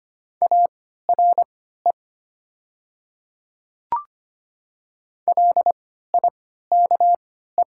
Morse code, heard as a keyed tone, about 25 wpm, spaced as for 14 wpm, 700 Hz. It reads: ARE LIKE
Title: 3x Character Spacing - Top 100 Words - 25wpm